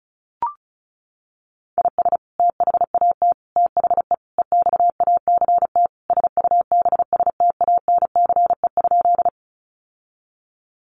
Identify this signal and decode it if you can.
Morse code, heard as a keyed tone, about 34 wpm, 700 Hz. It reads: IS THAT THE EXACT SUBSTANCE?
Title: Sentences from Top 1500 Words 35wpm